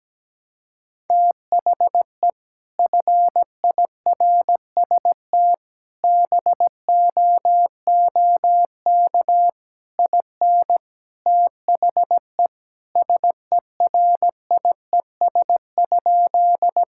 Morse code, heard as a keyed tone, about 17 wpm, 700 Hz. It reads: THE FIRST BOOK IN THE SERIES?
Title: Sentences from Top 1000 Words - Review Part 5 - 17wpm